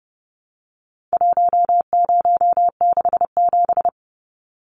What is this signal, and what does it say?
Morse code, heard as a keyed tone, about 30 wpm, 700 Hz. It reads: 1067
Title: Numbers - 4-Digit 30wpm